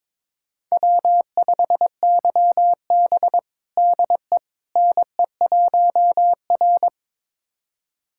Morse code, heard as a keyed tone, about 21 wpm, 700 Hz. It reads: W5YB DE NE1R
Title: USA Callsign DE Callsign - 4 Characters - 22wpm